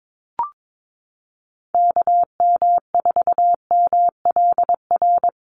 Morse code, heard as a keyed tone, about 22 wpm, 700 Hz. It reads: KM4MLR
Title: USA Call Signs 22wpm